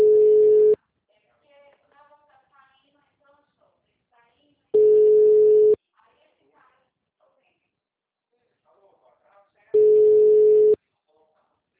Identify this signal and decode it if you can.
Ringback tone (Europe, 425 Hz), with speech.
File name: out- -2044- - - .67106